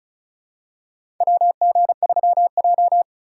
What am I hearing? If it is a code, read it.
Morse code, heard as a keyed tone, about 36 wpm, 700 Hz. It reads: WG3J
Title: USA Call Signs - 4 Characters - 4x Character Spacing - 35wpm